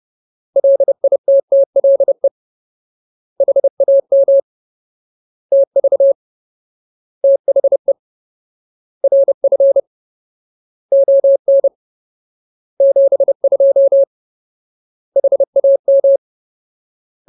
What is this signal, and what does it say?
Morse code, heard as a keyed tone, about 30 wpm, 550 Hz. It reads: LITTLE HAM TV THE RF ON 72 HAM
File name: Ultimate Word List / 30WPM Ultimate